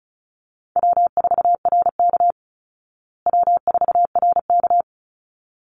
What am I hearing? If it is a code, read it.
Morse code, heard as a keyed tone, about 34 wpm, 700 Hz. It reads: W4RK W4RK
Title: Easy Call Signs Repeated 2x - 35wpm